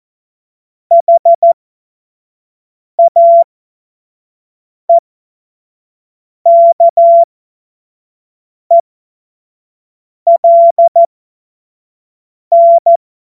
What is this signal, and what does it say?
Morse code, heard as a keyed tone, about 14 wpm, spaced as for 5 wpm, 680 Hz. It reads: HAEKELN
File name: Morsekurs / Witze / Witz 05. Oktober 2021, 25 F70